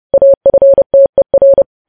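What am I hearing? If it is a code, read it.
Morse code, heard as a keyed tone, about 30 wpm, 550 Hz. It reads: AFTER
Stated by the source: News headlines sent in Morse code at 30.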